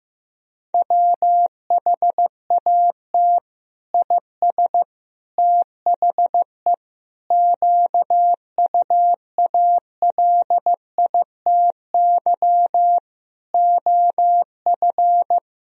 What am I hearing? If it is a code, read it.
Morse code, heard as a keyed tone, about 15 wpm, 700 Hz. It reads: WHAT IS THE QUALITY OF
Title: Sentences from Top 1000 Words - Review Part 4 - 15wpm